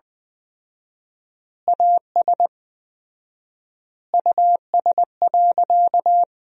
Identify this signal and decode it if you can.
Morse code, heard as a keyed tone, about 20 wpm, 700 Hz. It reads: AS US.
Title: 4x Word-Spacing - Sentences from Top 500 Words - 20wpm